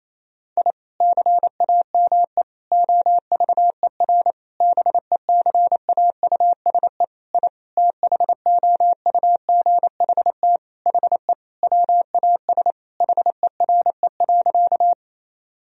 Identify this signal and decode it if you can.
Morse code, heard as a keyed tone, about 27 wpm, 700 Hz. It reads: I CAME OVER BECAUSE I THOUGHT HE WAS HERE.
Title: Sentences from Top 1000 Words - Review Part 3 - 28wpm